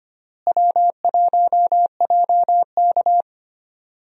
Morse code, heard as a keyed tone, about 25 wpm, 700 Hz. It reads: W1JK
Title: USA Call Signs - 4 Characters - Repeated 3x - 25wpm